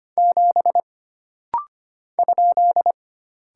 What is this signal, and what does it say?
Morse code, heard as a keyed tone, about 25 wpm, 700 Hz. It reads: 7 ?
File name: Single Character (Letter:Number:Punctuation:Prosign) - Rapid-Fire - 25wpm